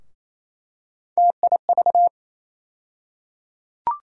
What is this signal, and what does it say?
Morse code, heard as a keyed tone, about 27 wpm, 700 Hz. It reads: TIV
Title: USA Call Sign Postfixes - 3 Letters 28wpm